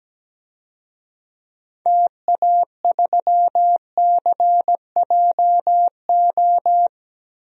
Morse code, heard as a keyed tone, about 17 wpm, 700 Hz. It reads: TA3CJO
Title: DXCC-100 Call Signs - 6 Characters - Repeated 3x - 17wpm